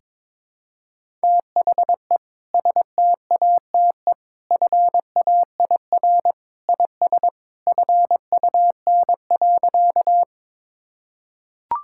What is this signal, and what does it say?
Morse code, heard as a keyed tone, about 22 wpm, 700 Hz. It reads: THE STATE FAIR IS FUN.
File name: practice / 22wpm / Sentences with 2-Word Phrases - 22wpm